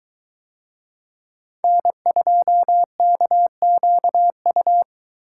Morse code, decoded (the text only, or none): N2KQU